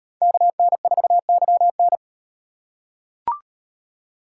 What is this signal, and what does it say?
Morse code, heard as a keyed tone, about 38 wpm, 700 Hz. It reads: KN4YN